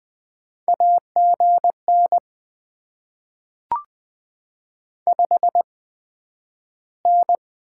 Morse code, heard as a keyed tone, about 20 wpm, 700 Hz. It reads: A G N 5 N
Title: Top 20 QSO Elements - 8x Character Spacing - 20wpm